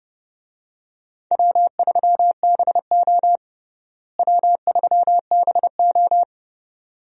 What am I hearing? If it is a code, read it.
Morse code, heard as a keyed tone, about 31 wpm, 700 Hz. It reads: W3BO W3BO
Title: USA Call Signs - 4 Characters - Repeated 2x - 30wpm